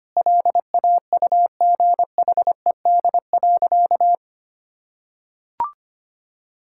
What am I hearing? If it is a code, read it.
Morse code, heard as a keyed tone, about 25 wpm, 700 Hz. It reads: LAUGHED.
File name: practice / 25wpm / Sentences with 2-Word Phrases - 25wpm